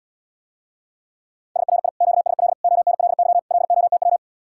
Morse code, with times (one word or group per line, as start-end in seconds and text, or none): 1.6-1.9 SRI
2.0-2.5 QSL
2.6-3.4 ONLY
3.5-4.2 LOTW